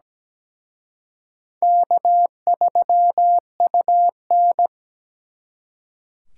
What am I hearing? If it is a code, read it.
Morse code, heard as a keyed tone, about 20 wpm, 700 Hz. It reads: K3UN
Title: Speed Racing - USA 1x2 Call Signs 17wpm